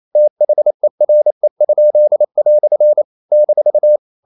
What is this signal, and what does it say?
Morse code, heard as a keyed tone, about 28 wpm, 600 Hz. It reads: THERE?" -
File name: CW / Practice / EnglishQuotations0128